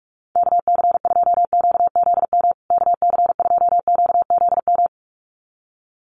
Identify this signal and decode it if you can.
Morse code, heard as a keyed tone, about 44 wpm, 700 Hz. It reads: KC2QZM KC2QZM
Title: USA Call Signs Repeated 2x 45wpm